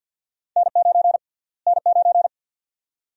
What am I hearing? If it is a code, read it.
Morse code, heard as a keyed tone, about 53 wpm, 700 Hz. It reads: N9 N9
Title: USA Call Sign Prefixes - Repeated 2x - 50wpm